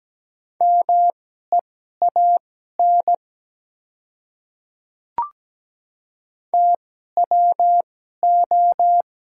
Morse code, heard as a keyed tone, about 17 wpm, spaced as for 12 wpm, 700 Hz. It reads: MEAN TWO
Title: 2x Character Spacing - Top 200 Words - 17wpm